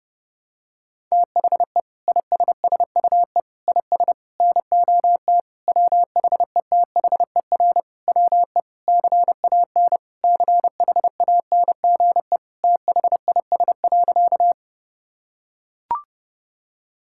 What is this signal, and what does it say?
Morse code, heard as a keyed tone, about 30 wpm, 700 Hz. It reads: THE ISSUE IS NOT WHETHER WE CAN CHANGE THIS.
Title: Sentences from Top 1000 Words - Review Part 6 - 30wpm